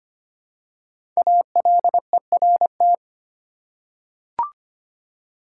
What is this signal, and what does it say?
Morse code, heard as a keyed tone, about 25 wpm, 700 Hz. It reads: ALERT
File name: 5 Letter Words - First 3 Letters Given - 25wpm